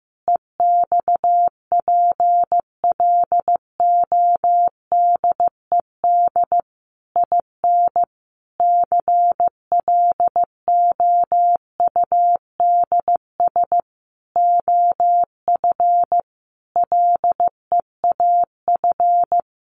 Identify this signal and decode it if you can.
Morse code, heard as a keyed tone, about 15 wpm, 700 Hz. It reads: EXPLODED IN CLOUDS OF LEAF
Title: A Sound Of Thunder 15wpm